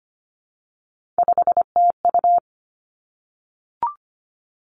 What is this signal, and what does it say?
Morse code, heard as a keyed tone, about 25 wpm, 700 Hz. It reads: 5TU
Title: Three Challenging Letter-Number Combinations 25wpm